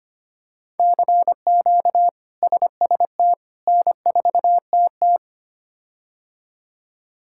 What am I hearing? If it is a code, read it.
Morse code, heard as a keyed tone, about 25 wpm, 700 Hz. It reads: CQ SST N4TT
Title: K1USN SST QSO Practice - 25wpm